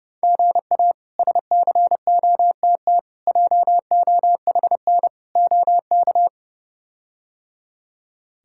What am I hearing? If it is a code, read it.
Morse code, heard as a keyed tone, about 31 wpm, 700 Hz. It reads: GA SCOTT JOHN OK
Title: K1USN SST QSO Practice - 30wpm